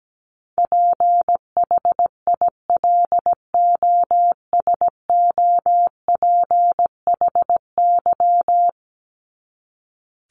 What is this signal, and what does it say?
Morse code, heard as a keyed tone, about 17 wpm, 700 Hz. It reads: PHILOSOPHY
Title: Top 751-1000 Words 17wpm